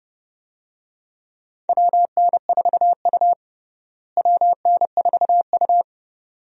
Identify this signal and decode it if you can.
Morse code, heard as a keyed tone, about 32 wpm, 700 Hz. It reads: WN4U WN4U
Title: USA Call Signs - 4 Characters - Repeated 2x - 30wpm